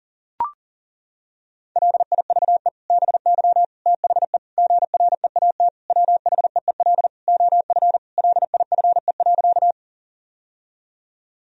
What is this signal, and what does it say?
Morse code, heard as a keyed tone, about 40 wpm, 700 Hz. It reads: LIVE BY THE GREAT WHEEL OF LIFE.
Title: Sentences from Top 1000 Words - Review Part 3 - 40wpm